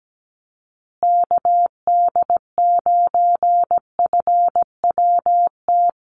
Morse code, heard as a keyed tone, about 17 wpm, 700 Hz. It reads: KD9FWT
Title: USA Call Signs Repeated 2x 17wpm